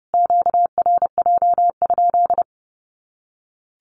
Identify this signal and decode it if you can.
Morse code, heard as a keyed tone, about 30 wpm, 700 Hz. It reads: QRJ?
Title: Q-Codes Repeated 3x 30wpm